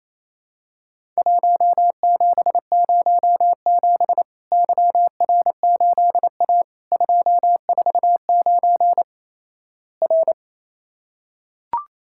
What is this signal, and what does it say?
Morse code, heard as a keyed tone, about 27 wpm, 700 Hz. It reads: 1707 YR8A 249 R
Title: Worked All Europe DX Contest - QTC Exchange - 28wpm